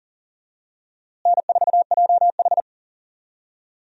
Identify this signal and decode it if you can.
Morse code, heard as a keyed tone, about 39 wpm, 700 Hz. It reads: N4JH
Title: USA Call Signs - 4 Characters - Repeated 3x - 40wpm